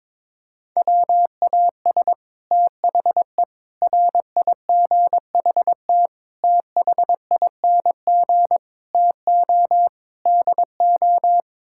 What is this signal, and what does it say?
Morse code, heard as a keyed tone, about 22 wpm, 700 Hz. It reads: WAS THE RIGHT THING TO DO
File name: practice / 22wpm / Sets of 6 Words - 22wpm